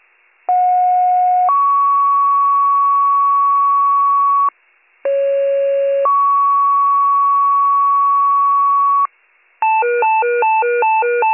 Two-tone page, heard as a one-second tone, then a three-second tone, about 730 then 1120 Hz, about 550 then 1120 Hz. It is heard from 0.5 to 9.1 s.